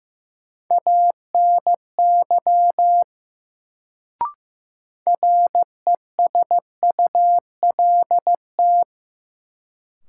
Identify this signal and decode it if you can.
Morse code, heard as a keyed tone, about 15 wpm, 700 Hz. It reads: ANY RESULT
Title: Sets of 3 Words - New Word Drill - 15wpm